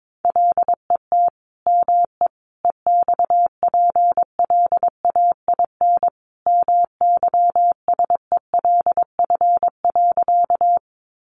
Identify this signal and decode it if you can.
Morse code, heard as a keyed tone, about 22 wpm, 700 Hz. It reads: LET ME EXPLAIN MYSELF.
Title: Sentences from Top 1300 Words 22wpm